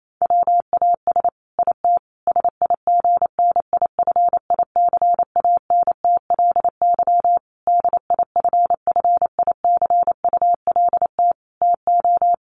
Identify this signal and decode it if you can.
Morse code, heard as a keyed tone, about 28 wpm, 700 Hz. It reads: WAS IT SIGNIFICANTLY DIFFICULT TO
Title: Sentences from Top 1500 Words 28wpm